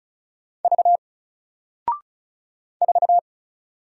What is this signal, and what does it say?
Morse code, heard as a keyed tone, about 36 wpm, 700 Hz. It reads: V 4
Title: Single Letter-Number 35wpm